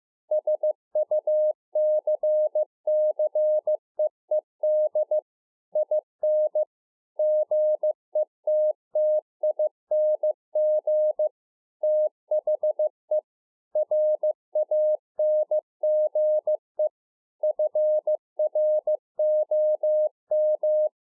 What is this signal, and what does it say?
Morse code, heard as a keyed tone, about 15 wpm, 610 Hz. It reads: SUCCEED IN GETTING THE RANGE FROM
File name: 2011-04-mp3 / A SMALL TRANCEIVER 2 15WPM